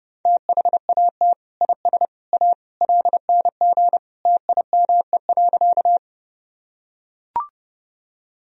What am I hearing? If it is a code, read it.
Morse code, heard as a keyed tone, about 30 wpm, 700 Hz. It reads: THAT IS A LNG TIME.
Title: Sentences from Top 100 Words and Abbreviations 30wpm